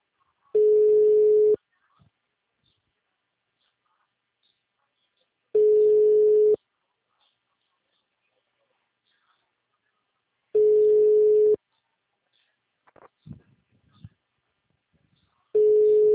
Ringback tone (Europe, 425 Hz).